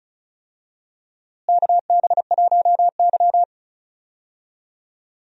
Morse code, heard as a keyed tone, about 36 wpm, 700 Hz. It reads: KB1Y
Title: USA Call Signs - 4 Characters - Repeated 3x - 35wpm